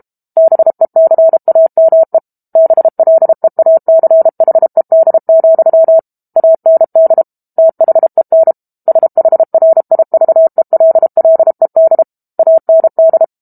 Morse code, heard as a keyed tone, about 32 wpm, 650 Hz. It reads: BECAME BLEACHED, AND THEN SHRIVELLED AND
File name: Book Two / 06-Chapter 6 - 32 WPM